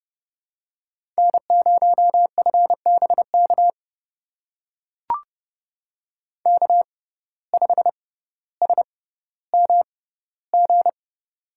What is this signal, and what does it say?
Morse code, heard as a keyed tone, about 30 wpm, spaced as for 15 wpm, 700 Hz. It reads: N0FBK K5SMG
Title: USA Call Signs - 5 Characters - 6x Character Spacing - 30wpm